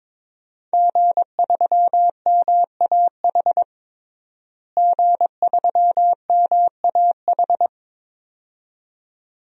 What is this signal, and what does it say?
Morse code, heard as a keyed tone, about 22 wpm, 700 Hz. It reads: G3MAH G3MAH
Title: DXCC-100 Call Signs - 5 Characters - Repeated 2x - 22wpm